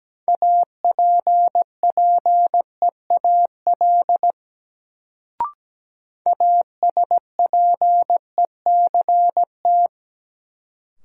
Morse code, heard as a keyed tone, about 17 wpm, 700 Hz. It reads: APPEAL ASPECT
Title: Sentences from Top 1300 Words - New Word Drill - 17wpm